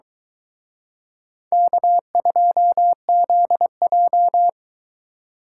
Morse code, heard as a keyed tone, about 22 wpm, 700 Hz. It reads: K2ZJ